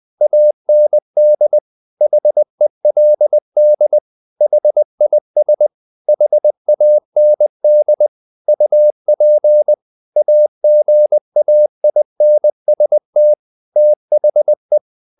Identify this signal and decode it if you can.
Morse code, heard as a keyed tone, about 20 wpm, 600 Hz. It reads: AND HELD HIS HAND UP AGAINST THE
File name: the old man and the sea / the old man and the sea part 1 20wpm